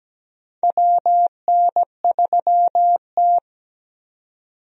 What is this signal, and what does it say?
Morse code, heard as a keyed tone, about 17 wpm, 700 Hz. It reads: WN3T
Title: USA Call Signs - 4 Characters - 17wpm